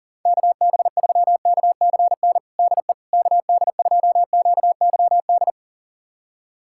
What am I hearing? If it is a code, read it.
Morse code, heard as a keyed tone, about 40 wpm, 700 Hz. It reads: KB3KCN DE KD2QYD
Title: USA Callsign DE Callsign - 6 Characters - 40wpm